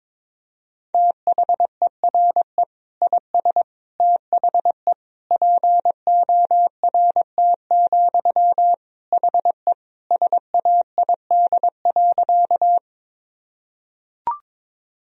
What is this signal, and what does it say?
Morse code, heard as a keyed tone, about 22 wpm, 700 Hz. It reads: THERE IS THE PORT, HE SAID.
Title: Sentences from Top 1000 Words - Review Part 1 - 22wpm